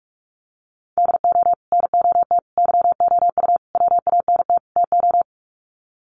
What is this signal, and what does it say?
Morse code, heard as a keyed tone, about 44 wpm, 700 Hz. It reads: DO NOT YOU WANT TO
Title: Sets of 5 Words Encore2 45wpm